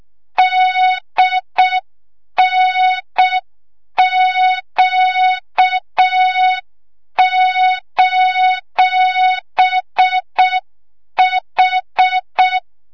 Morse code, heard as a keyed tone, about 6 wpm, 3700 Hz, with noise. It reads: DNQ:H
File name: Morze-HA2MN / hang / 25 GYAKORLAT